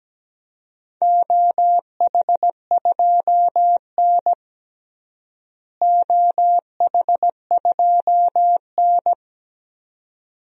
Morse code, heard as a keyed tone, about 17 wpm, 700 Hz. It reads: OH2N OH2N
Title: DXCC-100 Call Signs - 4 Characters - Repeated 2x - 17wpm